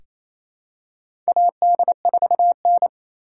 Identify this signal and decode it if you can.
Morse code, heard as a keyed tone, about 28 wpm, 700 Hz. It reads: AD4N
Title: USA Call Signs - 4 Characters - 4x Character Spacing - 28wpm